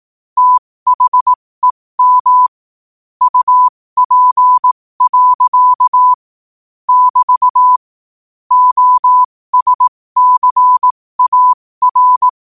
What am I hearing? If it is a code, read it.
Morse code, heard as a keyed tone, about 18 wpm, spaced as for 15 wpm, 1000 Hz. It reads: THEM UP. <BT> OSCAR
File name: qotdfiles / 15wpm